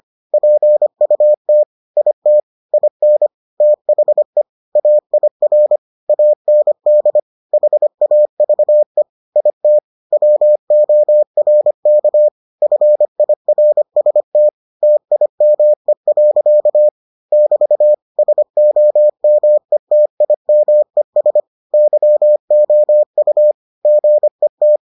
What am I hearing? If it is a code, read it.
Morse code, heard as a keyed tone, about 25 wpm, 600 Hz. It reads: PUT IT IN THE AIR AND HAVE IT WORK FIRST TIME. <BT> SOMETIMES YOU GET